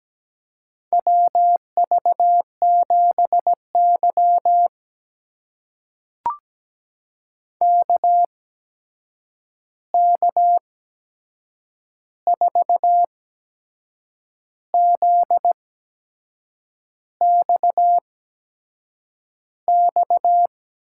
Morse code, heard as a keyed tone, about 17 wpm, spaced as for 6 wpm, 700 Hz. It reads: WV7Y KK4ZXX
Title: USA Call Signs - 8x Character Spacing - 17wpm